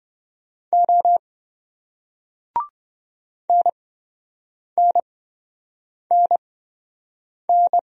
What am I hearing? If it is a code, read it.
Morse code, heard as a keyed tone, about 28 wpm, spaced as for 7 wpm, 700 Hz. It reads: O NNNN